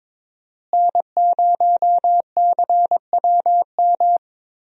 Morse code, heard as a keyed tone, about 22 wpm, 700 Hz. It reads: N0CWM
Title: USA Call Signs - 5 Characters - Repeated 3x - 22wpm